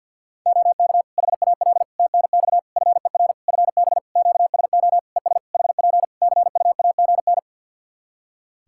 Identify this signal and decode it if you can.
Morse code, heard as a keyed tone, about 51 wpm, 700 Hz. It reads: OK HAL TNX FER FB QSO ES HP CUAGN
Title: Practice QSO - Exchange 4 - Ending - 50wpm